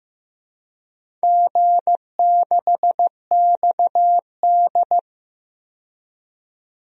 Morse code, heard as a keyed tone, about 15 wpm, 700 Hz. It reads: G6XD
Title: DXCC-100 Call Signs - 4 Characters - 15wpm